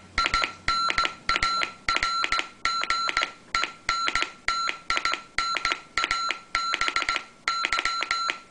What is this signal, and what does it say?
Morse code, heard as a keyed tone, about 17 wpm, 1360 Hz, with noise. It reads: IN ARGENTINA BY